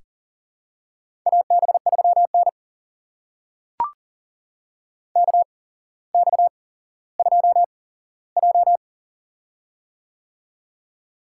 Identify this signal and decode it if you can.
Morse code, heard as a keyed tone, about 40 wpm, spaced as for 17 wpm, 700 Hz. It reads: AB3N KX2J
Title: USA Call Signs - 4 Characters - 8x Character Spacing - 40wpm